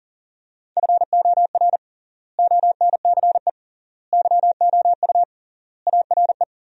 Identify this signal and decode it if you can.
Morse code, heard as a keyed tone, about 40 wpm, 700 Hz. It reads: FOR ONCE YOU ARE
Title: 3x Word-Spacing - Sentences from Top 300 Words - 40wpm